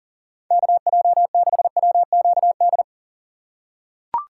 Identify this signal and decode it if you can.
Morse code, heard as a keyed tone, about 40 wpm, 700 Hz. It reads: KJ6WQD
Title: USA Call Signs - 6 Characters - 3x Character Spacing - 40wpm